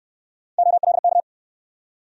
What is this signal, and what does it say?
Morse code, heard as a keyed tone, about 64 wpm, 700 Hz. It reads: OLK